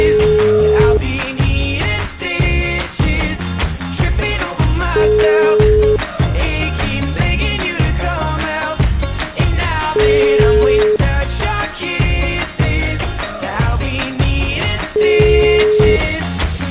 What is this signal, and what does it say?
Ringback tone (Europe, 425 Hz), with noise.